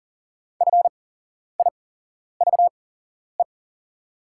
Morse code, heard as a keyed tone, about 40 wpm, spaced as for 11 wpm, 700 Hz. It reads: FIVE
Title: 8x Character Spacing - Top 500 Words - 40wpm